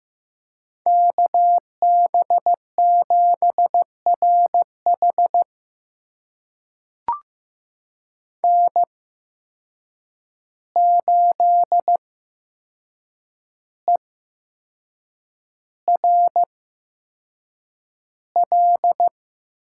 Morse code, heard as a keyed tone, about 15 wpm, spaced as for 6 wpm, 700 Hz. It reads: KB7RH N8ERL
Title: USA Call Signs - 8x Character Spacing - 15wpm